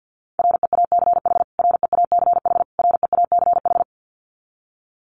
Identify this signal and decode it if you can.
Morse code, heard as a keyed tone, about 50 wpm, 700 Hz. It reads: REACH REACH REACH
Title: Top 500 Words Repeated 50wpm